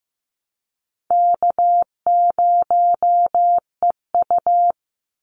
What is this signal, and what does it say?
Morse code, heard as a keyed tone, about 15 wpm, 700 Hz. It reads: K0EU
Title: Easy Call Signs Repeated 2x - 15wpm